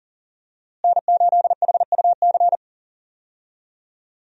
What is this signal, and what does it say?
Morse code, heard as a keyed tone, about 39 wpm, 700 Hz. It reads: N8HUC
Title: USA Call Signs - 5 Characters - Repeated 3x - 40wpm